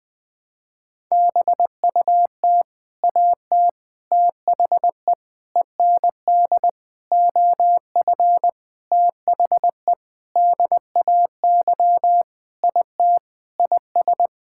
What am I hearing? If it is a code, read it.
Morse code, heard as a keyed tone, about 20 wpm, 700 Hz. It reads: BUT AT THE END OF THE DAY IT IS
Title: Sentences with 8-Word Phrases - 20wpm